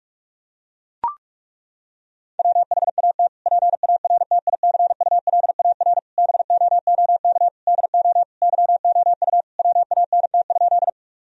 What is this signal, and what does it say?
Morse code, heard as a keyed tone, about 42 wpm, 700 Hz. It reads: WHAT PARTICULAR BOOK DO YOU WANT?
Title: Sentences from Top 1000 Words - Review Part 4 - 45wpm